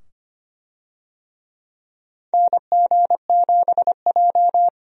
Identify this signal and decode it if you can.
Morse code, heard as a keyed tone, about 25 wpm, 700 Hz. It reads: NG7J
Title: USA Call Signs - 4 Characters - 25wpm